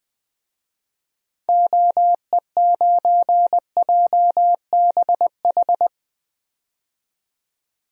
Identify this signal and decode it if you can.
Morse code, heard as a keyed tone, about 20 wpm, 700 Hz. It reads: OE9JBH
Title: DXCC-100 Call Signs - 6 Characters - 20wpm